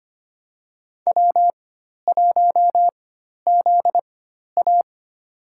Morse code, heard as a keyed tone, about 25 wpm, spaced as for 12 wpm, 690 Hz. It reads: W1ZA
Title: USA Call Signs - 4 Characters - 4x Character Spacing - 25wpm